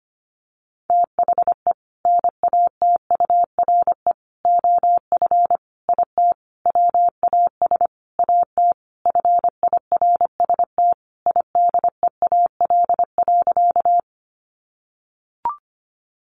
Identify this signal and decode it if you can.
Morse code, heard as a keyed tone, about 25 wpm, 700 Hz. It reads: THE NATURE OF IT WAS AT FIRST IDEAL.
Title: Sentences from Top 1400 Words 25wpm